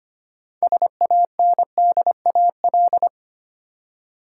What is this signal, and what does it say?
Morse code, heard as a keyed tone, about 25 wpm, 700 Hz. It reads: SANDAL
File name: practice / 25wpm / Easy Words with Hints - 25wpm